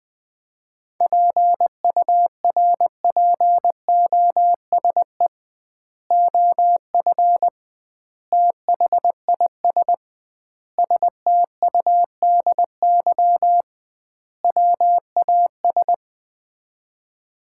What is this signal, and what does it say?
Morse code, heard as a keyed tone, about 20 wpm, 700 Hz. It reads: PURPOSE OF THIS STUDY WAS
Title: 2x Word-Spacing - Sets of 5 Words - 20wpm